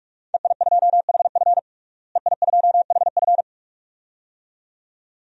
Morse code, heard as a keyed tone, about 46 wpm, 700 Hz. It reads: EI2HF EI2HF
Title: DXCC-100 Call Signs - 5 Characters - Repeated 2x - 45wpm